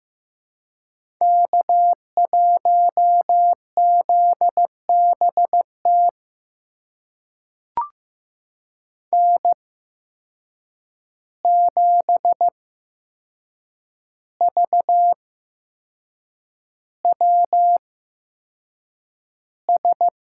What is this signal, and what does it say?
Morse code, heard as a keyed tone, about 15 wpm, spaced as for 6 wpm, 700 Hz. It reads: K1ZBT N7VWS
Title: USA Call Signs - 5 Characters - 8x Character Spacing - 15wpm